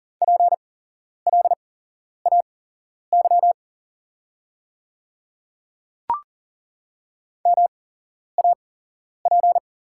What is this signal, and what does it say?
Morse code, heard as a keyed tone, about 41 wpm, spaced as for 11 wpm, 700 Hz. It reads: PLAY MAP
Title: 8x Character Spacing - Top 500 Words - 40wpm